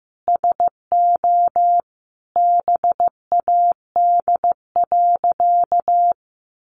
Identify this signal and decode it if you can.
Morse code, heard as a keyed tone, about 15 wpm, 700 Hz. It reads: SO BAD.
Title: Sentences from Top 700 Words 15wpm